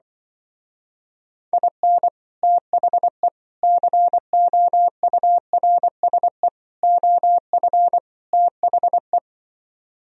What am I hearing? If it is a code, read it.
Morse code, heard as a keyed tone, about 25 wpm, 700 Hz. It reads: IN THE COURSE OF THE
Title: Speed-Racing - Sets of 5 Words 20wpm